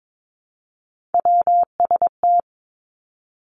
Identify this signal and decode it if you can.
Morse code, heard as a keyed tone, about 22 wpm, 700 Hz. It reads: WST
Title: Three Challenging Letter-Number Combinations 22wpm